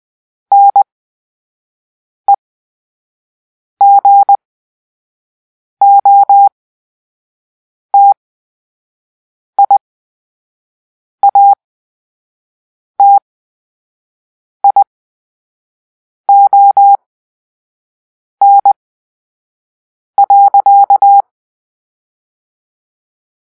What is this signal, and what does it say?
Morse code, heard as a keyed tone, about 20 wpm, spaced as for 5 wpm, 800 Hz. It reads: NEGOTIATION.
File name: Grp-3 Drill-4